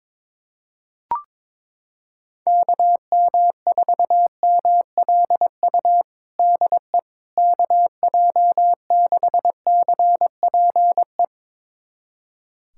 Morse code, heard as a keyed tone, about 22 wpm, 700 Hz. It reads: KM4MLU DE KJ6CPE
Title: USA Callsign DE Callsign 22wpm